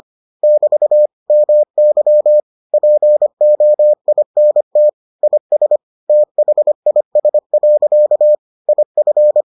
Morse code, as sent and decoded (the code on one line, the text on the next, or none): -...- / -- -.-- / .--. --- .. -. - / .. ... / - .... .. ... .-.-.- / .. ..-.
<BT> MY POINT IS THIS. IF